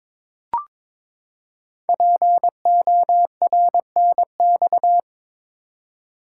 Morse code, heard as a keyed tone, about 22 wpm, 700 Hz. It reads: PORNX
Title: Code Groups - 5 Letters - 22wpm